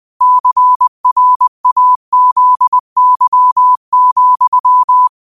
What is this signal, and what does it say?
Morse code, heard as a keyed tone, about 20 wpm, 1000 Hz. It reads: CRAZY,
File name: qotdfiles / 20wpm